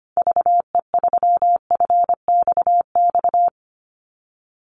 Morse code, heard as a keyed tone, about 25 wpm, 700 Hz. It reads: VE3FXX
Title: Call Signs Repeated 2x - 25wpm